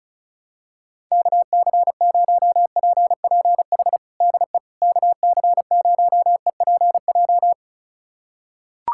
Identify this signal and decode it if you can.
Morse code, heard as a keyed tone, about 34 wpm, 700 Hz. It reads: KC0PPH DE KC0EPJ
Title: USA Callsign DE Callsign 35wpm